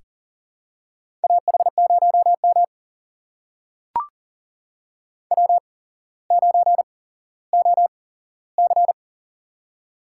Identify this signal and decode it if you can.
Morse code, heard as a keyed tone, about 40 wpm, spaced as for 17 wpm, 700 Hz. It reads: AH0M W9OC
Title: USA Call Signs - 4 Characters - 8x Character Spacing - 40wpm